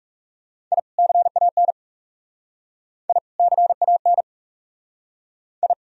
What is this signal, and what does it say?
Morse code, heard as a keyed tone, about 40 wpm, 700 Hz. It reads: I CAN I CAN I